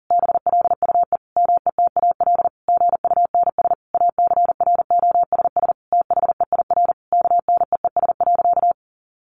Morse code, heard as a keyed tone, about 40 wpm, 700 Hz. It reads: BLUE METAL GUNS ACROSS THEIR KNEES.
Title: A Sound Of Thunder 40wpm